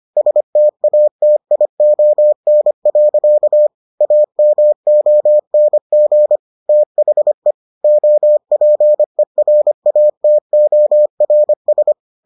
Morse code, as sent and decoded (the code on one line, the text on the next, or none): ... - .- - .. --- -. .-.-.- / .- -- --- -. --. / - .... / --- .--. .-. .- - --- .-. ...
STATION. AMONG THE OPERATORS